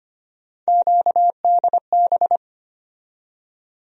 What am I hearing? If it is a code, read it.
Morse code, heard as a keyed tone, about 25 wpm, 700 Hz. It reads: QDB
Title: USA Call Sign Postfixes - 3 Letters 25wpm